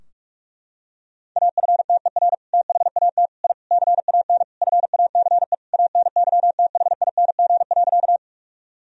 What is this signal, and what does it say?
Morse code, heard as a keyed tone, about 46 wpm, 700 Hz. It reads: AFTER THAT I CAN FACE ANYTHING.
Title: Sentences from Top 1000 Words - Review Part 2 - 45wpm